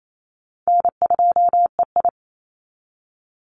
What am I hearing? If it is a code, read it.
Morse code, heard as a keyed tone, about 28 wpm, 700 Hz. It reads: N2EI
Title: Call Signs Repeated 3x - 28wpm